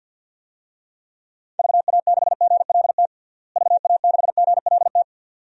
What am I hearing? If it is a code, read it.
Morse code, heard as a keyed tone, about 53 wpm, 700 Hz. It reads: VA6GLT VA6GLT
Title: DXCC-100 Call Signs - 6 Characters - Repeated 2x - 50wpm